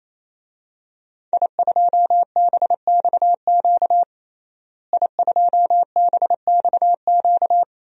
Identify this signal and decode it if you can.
Morse code, heard as a keyed tone, about 27 wpm, 700 Hz. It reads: I2BXQ I2BXQ
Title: DXCC-100 Call Signs - 5 Characters - Repeated 3x - 28wpm